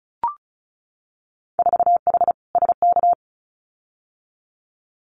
Morse code, heard as a keyed tone, about 34 wpm, 700 Hz. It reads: 4H SK
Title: Winter Field Day Exchanges Encore 35wpm